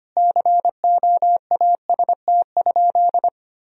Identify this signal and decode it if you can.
Morse code, heard as a keyed tone, about 25 wpm, 700 Hz. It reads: COAST?
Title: Sentences from Top 1000 Words - Review Part 4 - 25wpm